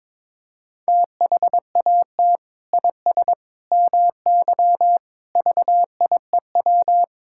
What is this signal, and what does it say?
Morse code, heard as a keyed tone, about 22 wpm, 700 Hz. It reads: THAT IS MY VIEW
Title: Sentences from Top 1000 Words - Review Part 6 - 22wpm